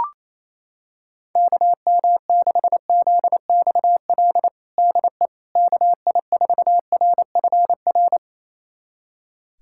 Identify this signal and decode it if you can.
Morse code, heard as a keyed tone, about 28 wpm, 700 Hz. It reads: KM6ZXL DE KI4RFR
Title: USA Callsign DE Callsign 28wpm